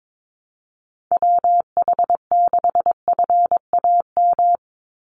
Morse code, heard as a keyed tone, about 22 wpm, 700 Hz. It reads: WH6FAM
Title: USA Call Signs 22wpm